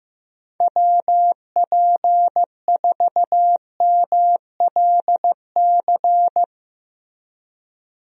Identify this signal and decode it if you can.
Morse code, heard as a keyed tone, about 15 wpm, 700 Hz. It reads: WP4MLC
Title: USA Call Signs - 6 Characters - Repeated 2x - 15wpm